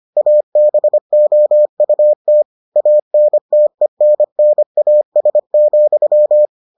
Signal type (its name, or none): Morse code